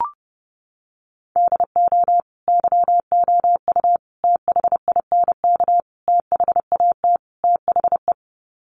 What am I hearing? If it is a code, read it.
Morse code, heard as a keyed tone, about 30 wpm, 700 Hz. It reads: DO YOU THINK THAT THE
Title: Sets of 5 Words Encore 30wpm